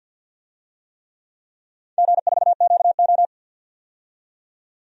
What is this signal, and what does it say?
Morse code, heard as a keyed tone, about 47 wpm, 700 Hz. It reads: G4QO
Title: DXCC-100 Call Signs - 4 Characters - 50wpm